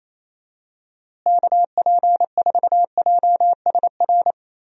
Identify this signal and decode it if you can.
Morse code, heard as a keyed tone, about 28 wpm, 700 Hz. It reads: KP4JSR
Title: USA Call Signs - 6 Characters - Repeated 2x - 28wpm